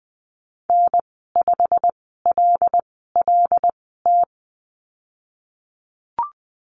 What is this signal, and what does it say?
Morse code, heard as a keyed tone, about 20 wpm, spaced as for 15 wpm, 700 Hz. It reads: N5LLT
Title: 147 Lesson 24g - Letter y - Call Signs 20wpm